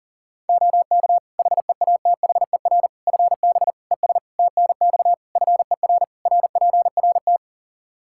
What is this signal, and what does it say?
Morse code, heard as a keyed tone, about 40 wpm, 700 Hz. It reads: OK HEATHER FB ES TNX FER RPRT
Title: K1USN SST QSO Practice 40wpm